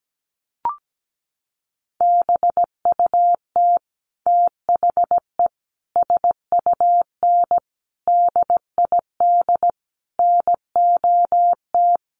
Morse code, heard as a keyed tone, about 17 wpm, 700 Hz. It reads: BUT THE SUN DID NOT
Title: Sentences from Top 1100 Words 17wpm